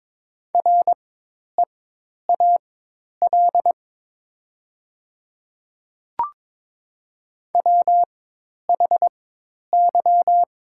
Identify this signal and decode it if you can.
Morse code, heard as a keyed tone, about 22 wpm, spaced as for 10 wpm, 700 Hz. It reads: REAL WHY
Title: 4x Character Spacing - Top 300 Words - 22wpm